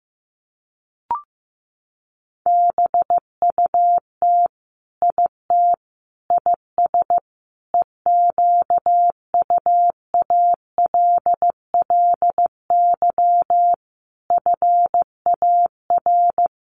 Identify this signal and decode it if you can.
Morse code, heard as a keyed tone, about 15 wpm, 700 Hz. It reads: BUT IT IS EQUALLY FAR
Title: Sentences from Top 1400 Words 15wpm